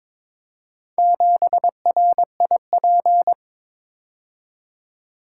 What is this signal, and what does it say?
Morse code, heard as a keyed tone, about 22 wpm, 700 Hz. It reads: 7RIP
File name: practice / 22wpm / Four Letter-Number Combinations - 22wpm